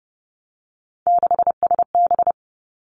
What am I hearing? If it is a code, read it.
Morse code, heard as a keyed tone, about 30 wpm, 700 Hz. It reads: 6SB
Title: Three Challenging Letter-Number Combinations 30wpm